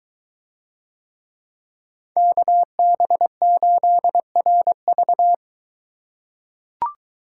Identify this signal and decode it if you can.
Morse code, heard as a keyed tone, about 21 wpm, 700 Hz. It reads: KB8RV